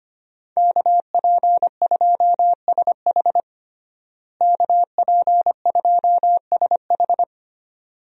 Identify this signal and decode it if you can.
Morse code, heard as a keyed tone, about 25 wpm, 700 Hz. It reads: KP2SH KP2SH
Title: DXCC-100 Call Signs - 5 Characters - Repeated 2x - 25wpm